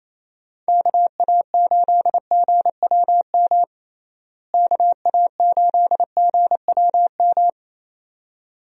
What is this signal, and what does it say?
Morse code, heard as a keyed tone, about 27 wpm, 700 Hz. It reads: KA8GWM KA8GWM
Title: USA Call Signs - 6 Characters - Repeated 2x - 28wpm